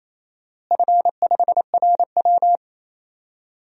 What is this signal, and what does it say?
Morse code, heard as a keyed tone, about 28 wpm, 700 Hz. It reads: F5RW
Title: DXCC-100 Call Signs - 4 Characters - Repeated 3x - 28wpm